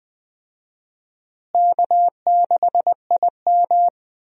Morse code, heard as a keyed tone, about 19 wpm, 700 Hz. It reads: K6IM
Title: USA Call Signs - 4 Characters - Repeated 3x - 20wpm